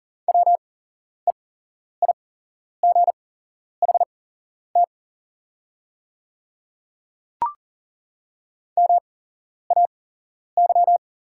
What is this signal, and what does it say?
Morse code, heard as a keyed tone, about 40 wpm, spaced as for 11 wpm, 690 Hz. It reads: WEIGHT MAY
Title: 8x Character Spacing - Top 500 Words - 40wpm